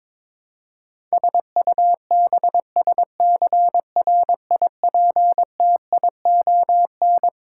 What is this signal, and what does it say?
Morse code, heard as a keyed tone, about 22 wpm, 700 Hz. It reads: SUBSCRIPTION
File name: Twelve Letter Words - 22wpm